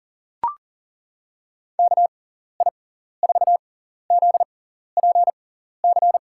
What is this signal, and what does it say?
Morse code, heard as a keyed tone, about 40 wpm, spaced as for 14 wpm, 700 Hz. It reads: KI4ZPC
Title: USA Call Signs - 6x Character Spacing - 40wpm